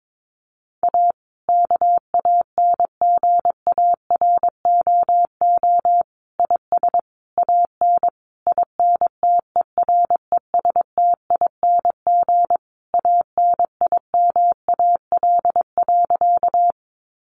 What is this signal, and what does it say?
Morse code, heard as a keyed tone, about 22 wpm, 700 Hz. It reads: A KANGAROO IS AN INTERESTING ANIMAL.
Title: Easy Phrases 22wpm